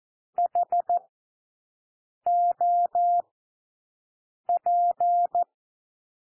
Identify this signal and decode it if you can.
Morse code, heard as a keyed tone, about 14 wpm, spaced as for 6 wpm, 700 Hz. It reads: HOP
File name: cwmp3 / W1 SOUND WORDS 14 wpm 0